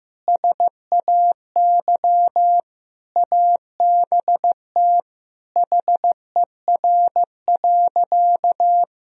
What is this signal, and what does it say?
Morse code, heard as a keyed tone, about 16 wpm, 700 Hz. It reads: SAY ABT HER.
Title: Sentences from Top 200 Words and Abbreviations 15wpm